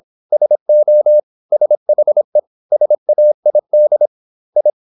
Morse code, heard as a keyed tone, about 26 wpm, 600 Hz. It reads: SO SHE SAID I